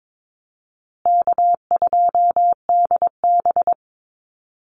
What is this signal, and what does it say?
Morse code, heard as a keyed tone, about 22 wpm, 700 Hz. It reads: K2DB
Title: Easy Call Signs Repeated 3x - 22wpm